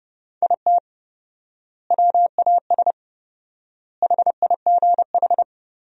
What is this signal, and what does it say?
Morse code, heard as a keyed tone, about 32 wpm, 700 Hz. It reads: IT WAS HIGH
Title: 4x Word-Spacing - Sentences from Top 200 Words - 30wpm